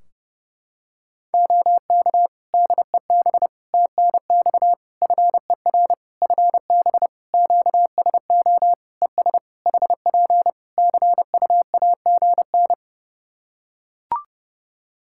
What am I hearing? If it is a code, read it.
Morse code, heard as a keyed tone, about 30 wpm, 700 Hz. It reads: OK DEB TNX FER FB QSO ES HP CUAGN
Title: Practice QSO - Exchange 4 - Ending - 30wpm